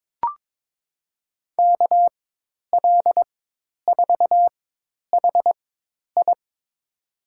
Morse code, heard as a keyed tone, about 23 wpm, spaced as for 10 wpm, 700 Hz. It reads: KL4HI
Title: USA Call Signs - 5 Characters - 4x Character Spacing - 22wpm